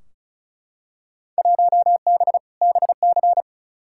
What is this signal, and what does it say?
Morse code, heard as a keyed tone, about 36 wpm, 700 Hz. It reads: 1B BC
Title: Field Day Exchanges - 35wpm